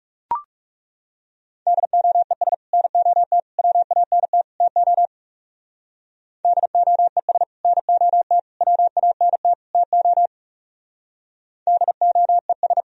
Morse code, heard as a keyed tone, about 38 wpm, 700 Hz. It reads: DOES NOT WANT TO DOES NOT WANT TO DOES